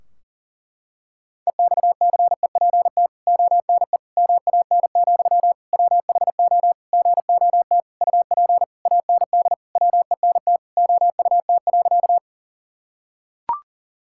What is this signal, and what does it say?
Morse code, heard as a keyed tone, about 40 wpm, 700 Hz. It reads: EXCEPT ONE MAN, WHO GOT UP AND WENT OUT.
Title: Sentences from Top 1600 Words 40wpm